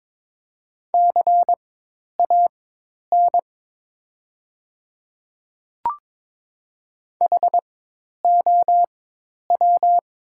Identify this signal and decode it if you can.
Morse code, heard as a keyed tone, about 22 wpm, spaced as for 10 wpm, 700 Hz. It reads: CAN HOW
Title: 4x Character Spacing - Top 100 Words - 22wpm